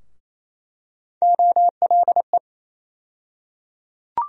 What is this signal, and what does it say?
Morse code, heard as a keyed tone, about 28 wpm, 700 Hz. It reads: OLE
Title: USA Call Sign Postfixes - 3 Letters 28wpm